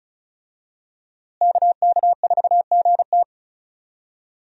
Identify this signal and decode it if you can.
Morse code, heard as a keyed tone, about 34 wpm, 700 Hz. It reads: KK4GT
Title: USA Call Signs - 5 Characters - Repeated 2x - 35wpm